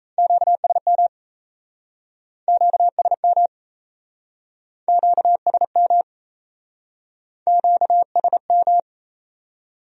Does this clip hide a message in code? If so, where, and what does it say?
Yes: Morse code from 0.2 to 8.8 s, heard as a keyed tone, about 37 wpm, 700 Hz. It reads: QSM QSM QSM QSM